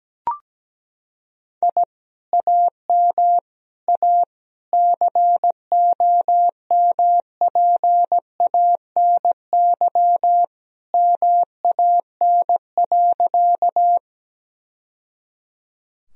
Morse code, heard as a keyed tone, about 17 wpm, 700 Hz. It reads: I AM A COMPANY MAN.
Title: Sentences from Top 1000 Words - Review Part 3 - 17wpm